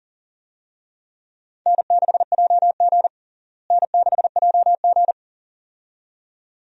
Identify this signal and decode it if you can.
Morse code, heard as a keyed tone, about 40 wpm, 700 Hz. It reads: N6JG N6JG
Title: USA Call Signs - 4 Characters - Repeated 2x - 40wpm